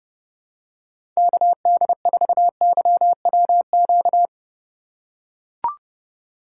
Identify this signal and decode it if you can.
Morse code, heard as a keyed tone, about 30 wpm, 700 Hz. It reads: KD4YWQ
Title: USA Call Signs - 6 Characters - 2x Character Spacing - 30wpm